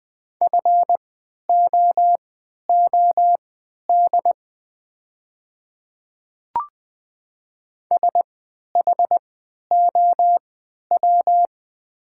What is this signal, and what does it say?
Morse code, heard as a keyed tone, about 20 wpm, spaced as for 11 wpm, 700 Hz. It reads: FOOD SHOW
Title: 3x Character Spacing - Top 500 Words - 20wpm